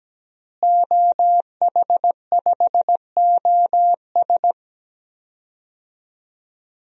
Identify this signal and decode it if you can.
Morse code, heard as a keyed tone, about 17 wpm, 700 Hz. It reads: OH5OS
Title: DXCC-100 Call Signs - 5 Characters - Repeated 2x - 17wpm